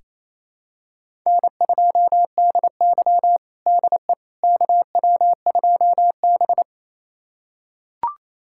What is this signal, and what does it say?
Morse code, heard as a keyed tone, about 28 wpm, 700 Hz. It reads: N2DY DE KW2B
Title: USA Callsign DE Callsign - 4 Characters - 28wpm